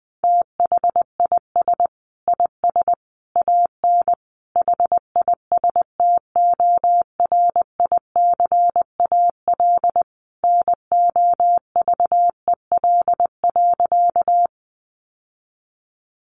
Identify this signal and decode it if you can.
Morse code, heard as a keyed tone, about 20 wpm, 700 Hz. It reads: THIS IS AN HISTORICAL NOVEL.
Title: Sentences from Top 1300 Words 20wpm